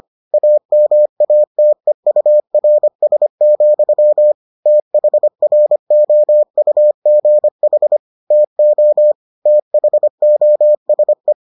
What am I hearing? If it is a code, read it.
Morse code, heard as a keyed tone, about 25 wpm, 600 Hz. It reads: AMATEURS, THROUGH TO THOSE